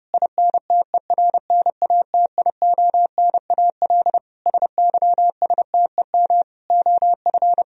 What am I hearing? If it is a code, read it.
Morse code, heard as a keyed tone, about 32 wpm, 700 Hz. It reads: INTERNATIONAL SYSTEM OF
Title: Sentences from Top 1000 Words - Review Part 4 - 30wpm